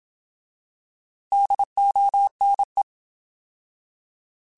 Morse code, heard as a keyed tone, about 27 wpm, 800 Hz. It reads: DONE